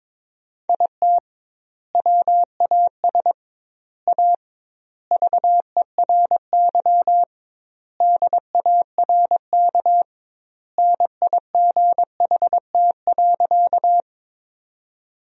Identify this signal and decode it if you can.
Morse code, heard as a keyed tone, about 22 wpm, 700 Hz. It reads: IT WAS A VERY DARK NIGHT.
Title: 2x Word-Spacing - Sentences from Top 500 Words - 22wpm